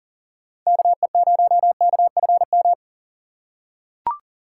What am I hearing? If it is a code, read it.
Morse code, heard as a keyed tone, about 40 wpm, 700 Hz. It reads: KE0KFM
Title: USA Call Signs - 6 Characters - 3x Character Spacing - 40wpm